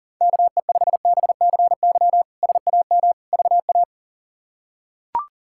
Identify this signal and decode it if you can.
Morse code, heard as a keyed tone, about 37 wpm, 700 Hz. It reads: KE5BCY SAM VA
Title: K1USN SST QSO Practice - 40wpm